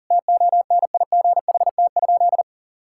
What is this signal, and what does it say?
Morse code, heard as a keyed tone, about 40 wpm, 700 Hz. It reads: TONIGHT?
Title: Sentences from Top 1600 Words 40wpm